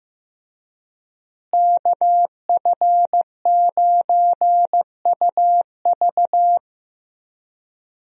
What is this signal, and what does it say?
Morse code, heard as a keyed tone, about 15 wpm, 700 Hz. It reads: KF9UV
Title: USA Call Signs - 5 Characters - Repeated 3x - 15wpm